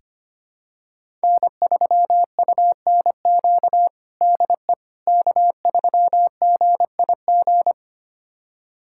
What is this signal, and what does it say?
Morse code, heard as a keyed tone, about 25 wpm, 700 Hz. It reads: N3UNQ DE K3GIG
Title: USA Callsign DE Callsign - 5 Characters - 25wpm